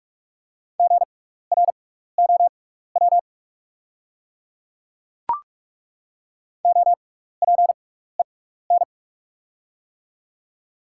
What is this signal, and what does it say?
Morse code, heard as a keyed tone, about 46 wpm, spaced as for 16 wpm, 690 Hz. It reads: GROW OPEN
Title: 6x Character Spacing - Top 500 Words - 45wpm